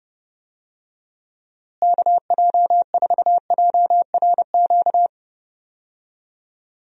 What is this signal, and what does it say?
Morse code, heard as a keyed tone, about 30 wpm, 700 Hz. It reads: KJ4JRQ
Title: USA Call Signs - 6 Characters - Repeated 2x - 30wpm